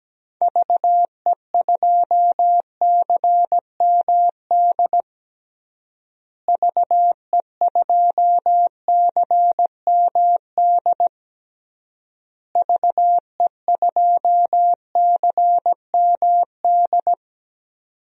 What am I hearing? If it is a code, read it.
Morse code, heard as a keyed tone, about 17 wpm, 700 Hz. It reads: VE2CMD VE2CMD VE2CMD
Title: DXCC-100 Call Signs - 6 Characters - Repeated 3x - 17wpm